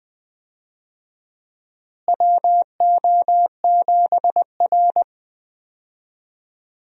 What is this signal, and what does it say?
Morse code, heard as a keyed tone, about 20 wpm, 700 Hz. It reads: WO7R
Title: USA Call Signs - 4 Characters - Repeated 3x - 20wpm